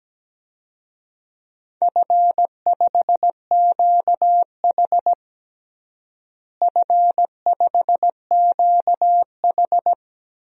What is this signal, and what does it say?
Morse code, heard as a keyed tone, about 17 wpm, 700 Hz. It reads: F5QH F5QH
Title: DXCC-100 Call Signs - 4 Characters - Repeated 3x - 17wpm